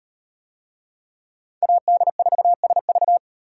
Morse code, heard as a keyed tone, about 38 wpm, 700 Hz. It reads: AD4SV